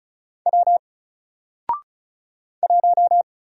Morse code, heard as a keyed tone, about 33 wpm, 700 Hz. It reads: W 1
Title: Single Letter-Number Alt 35wpm